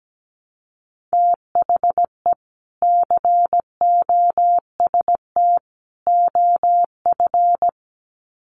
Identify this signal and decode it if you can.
Morse code, heard as a keyed tone, about 17 wpm, 700 Hz. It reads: THE COST OF
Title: Sets of 3 Words 17wpm